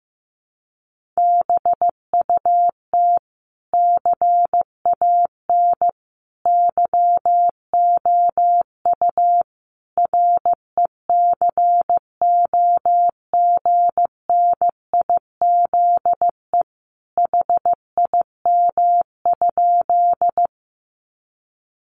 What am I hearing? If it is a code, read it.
Morse code, heard as a keyed tone, about 15 wpm, 700 Hz. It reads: BUT CAN YOU RECOGNIZE HIM?
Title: Sentences from Top 1400 Words 15wpm